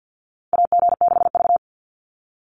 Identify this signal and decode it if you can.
Morse code, heard as a keyed tone, about 50 wpm, 700 Hz. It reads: AG6V
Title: NA SOTA Chaser Callsigns 50wpm